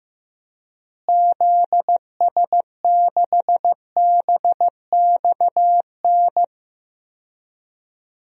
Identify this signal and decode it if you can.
Morse code, heard as a keyed tone, about 15 wpm, 700 Hz. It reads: ZS6BXN
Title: DXCC-100 Call Signs - 6 Characters - Repeated 3x - 15wpm